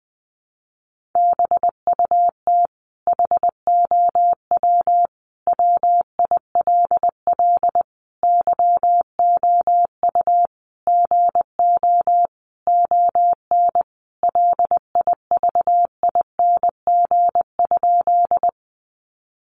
Morse code, heard as a keyed tone, about 19 wpm, 700 Hz. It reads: BUT HOW WILL YOU GO ON LIVING?
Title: Sentences from Top 1100 Words 20wpm